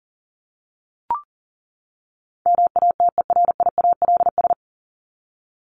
Morse code, heard as a keyed tone, about 37 wpm, 700 Hz. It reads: MATERIALS
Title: ES:S Suffix - Top 100 Words with Base Words 40wpm